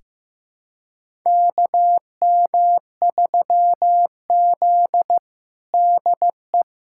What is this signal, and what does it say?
Morse code, heard as a keyed tone, about 15 wpm, 700 Hz. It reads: KM3Z DE
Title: USA Callsign DE Callsign - 4 Characters - 15wpm